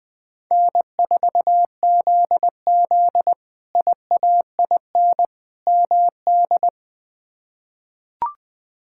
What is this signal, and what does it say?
Morse code, heard as a keyed tone, about 21 wpm, 700 Hz. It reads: N4ZZ IAIN MD
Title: K1USN SST QSO Practice - 20wpm